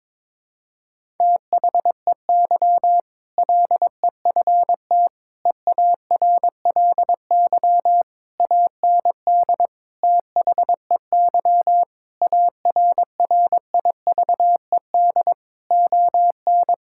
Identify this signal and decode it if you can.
Morse code, heard as a keyed tone, about 22 wpm, 700 Hz. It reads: THEY LEFT EARLY AND THEY ARRIVED ON
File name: practice / 22wpm / Sentences with 2-Word Phrases - 22wpm